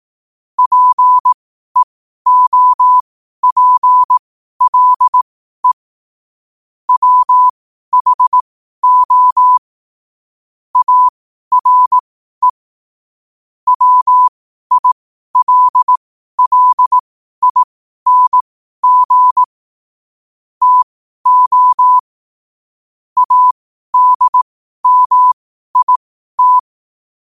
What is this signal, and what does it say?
Morse code, heard as a keyed tone, about 18 wpm, spaced as for 12 wpm, 1000 Hz. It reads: PEOPLE WHO ARE WILLING TO ADMIT
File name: qotdfiles / 12wpm